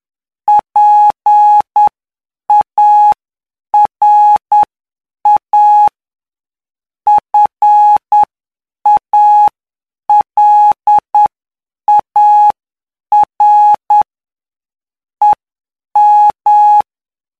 Morse code, heard as a keyed tone, about 9 wpm, spaced as for 7 wpm, 840 Hz. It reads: PARA FALAR EM
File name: cw / py2lcd / 19-PY2LCD-Li��o 28